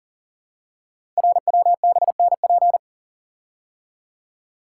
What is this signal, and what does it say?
Morse code, heard as a keyed tone, about 43 wpm, 700 Hz. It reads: RWBNP
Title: Code Groups - 5 Letters - 40wpm